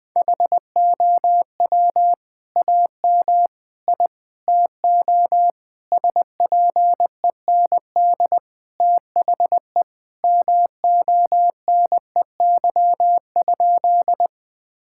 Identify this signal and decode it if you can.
Morse code, heard as a keyed tone, about 19 wpm, 700 Hz. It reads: HOW AM I TO SPEND THE MONEY?
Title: Sentences from Top 1000 Words - Review Part 5 - 20wpm